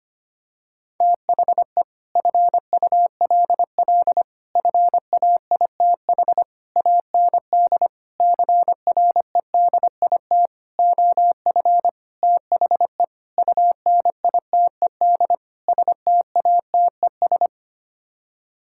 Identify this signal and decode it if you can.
Morse code, heard as a keyed tone, about 25 wpm, 700 Hz. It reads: THE FULL FAITH AND CREDIT OF THE UNITED STATES
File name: Sets of 9 Words - 25wpm